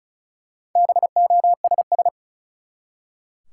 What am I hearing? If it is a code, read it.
Morse code, heard as a keyed tone, about 36 wpm, 700 Hz. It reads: BOSS
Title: Four Letter Words - 35wpm